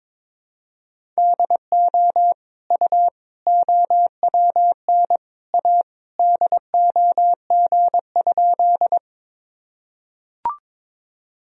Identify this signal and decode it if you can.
Morse code, heard as a keyed tone, about 22 wpm, 700 Hz. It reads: DO U OWN A DOG?
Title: Sentences from Top 400 Words and Abbreviations 22wpm